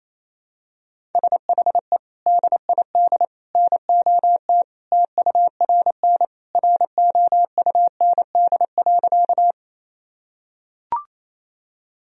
Morse code, heard as a keyed tone, about 28 wpm, 700 Hz. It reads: SHE DID NOT TURN ROUND.
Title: Sentences from Top 1000 Words - Review Part 1 - 28wpm